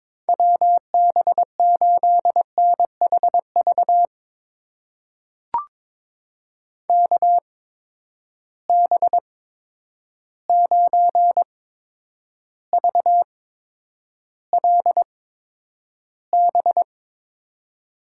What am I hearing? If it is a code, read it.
Morse code, heard as a keyed tone, about 22 wpm, spaced as for 9 wpm, 700 Hz. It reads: WB8NHV KB9VLB
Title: USA Call Signs - 6 Characters - 8x Character Spacing - 22wpm